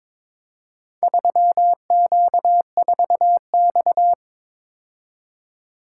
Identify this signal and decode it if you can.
Morse code, heard as a keyed tone, about 22 wpm, 700 Hz. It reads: 3Q4X
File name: practice / 22wpm / Four Letter-Number Combinations - 22wpm